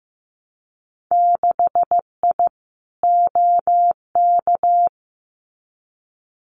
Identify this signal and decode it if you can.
Morse code, heard as a keyed tone, about 15 wpm, 700 Hz. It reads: 6I OK
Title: Winter Field Day Exchanges Encore 15wpm